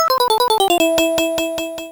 There is no code.